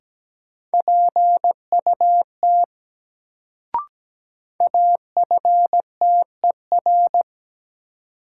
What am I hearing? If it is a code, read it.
Morse code, heard as a keyed tone, about 17 wpm, 700 Hz. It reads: PUT AFTER
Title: Sentences from Top 200 Words - New Word Drill - 17wpm